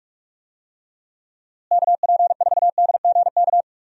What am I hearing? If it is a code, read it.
Morse code, heard as a keyed tone, about 44 wpm, 700 Hz. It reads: KP4DGK
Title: USA Call Signs - 6 Characters - Repeated 2x - 45wpm